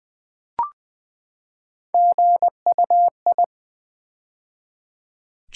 Morse code, heard as a keyed tone, about 20 wpm, 690 Hz. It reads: GUI
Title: Build A Word - 3 Letters At A Time - 8-Letter Words - 20wpm